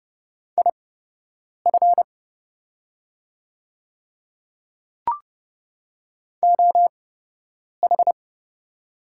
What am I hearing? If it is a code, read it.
Morse code, heard as a keyed tone, about 30 wpm, spaced as for 8 wpm, 700 Hz. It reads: IF OH
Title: 8x Character Spacing - Top 500 Words - 30wpm